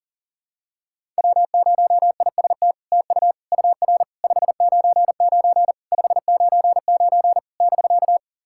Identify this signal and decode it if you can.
Morse code, heard as a keyed tone, about 43 wpm, 700 Hz. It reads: W0IST TU UR 599 599 <BK>
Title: SOTA QSO Practice 40wpm